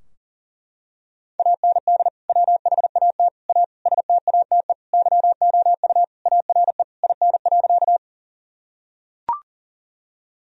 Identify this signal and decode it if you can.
Morse code, heard as a keyed tone, about 40 wpm, 700 Hz. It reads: AND WHAT A STATE YOU ARE IN.
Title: Sentences from Top 1000 Words - Review Part 2 - 40wpm